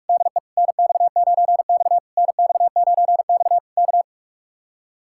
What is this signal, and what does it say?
Morse code, heard as a keyed tone, about 59 wpm, spaced as for 51 wpm, 700 Hz. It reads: DE NX9X NX9X K
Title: Speed-Racing Common QSO Phrases Encore - 45wpm